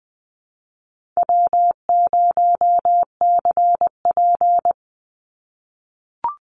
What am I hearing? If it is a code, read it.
Morse code, heard as a keyed tone, about 20 wpm, 700 Hz. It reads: W0CP
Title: NA SOTA Activator Callsigns 20wpm